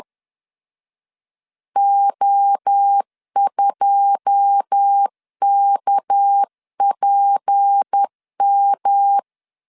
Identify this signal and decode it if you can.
Morse code, heard as a keyed tone, about 10 wpm, 800 Hz. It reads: O2KPM
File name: LEZIONI / 9 2 10wpm 2736